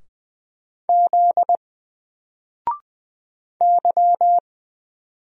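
Morse code, heard as a keyed tone, about 20 wpm, 700 Hz. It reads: Z Y
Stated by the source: Single letter and number Morse drill at 20 wpm, from N4LCD.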